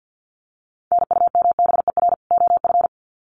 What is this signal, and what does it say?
Morse code, heard as a keyed tone, about 50 wpm, 700 Hz. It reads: NUMBER OF
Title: Sets of 2 Words 50wpm